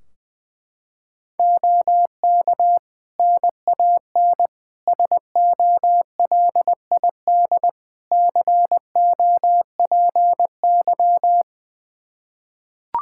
Morse code, heard as a keyed tone, about 21 wpm, 700 Hz. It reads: OK NAN SOLID COPY
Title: Practice QSO - Exchange 3 - More Info - 20wpm